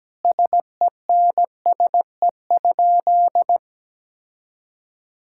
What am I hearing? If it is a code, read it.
Morse code, heard as a keyed tone, about 17 wpm, 700 Hz. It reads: SENSE?
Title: Sentences from Top 1000 Words - Review Part 5 - 17wpm